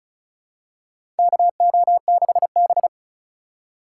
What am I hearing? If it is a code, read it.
Morse code, heard as a keyed tone, about 36 wpm, 700 Hz. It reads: KO6B
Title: USA Call Signs - 4 Characters - 3x Character Spacing - 35wpm